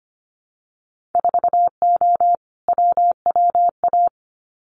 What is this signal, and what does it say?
Morse code, heard as a keyed tone, about 25 wpm, 700 Hz. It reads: 4O WWA
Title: Winter Field Day Exchanges 25wpm